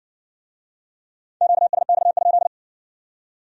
Morse code, heard as a keyed tone, about 62 wpm, 700 Hz. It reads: QSX?